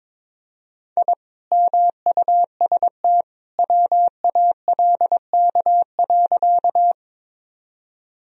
Morse code, heard as a keyed tone, about 22 wpm, 700 Hz. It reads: I MUST WALK.
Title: Sentences from Top 1000 Words - Review Part 1 - 22wpm